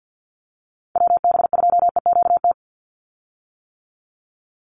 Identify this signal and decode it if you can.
Morse code, heard as a keyed tone, about 50 wpm, 700 Hz. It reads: WB2EQT
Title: USA Call Signs 50wpm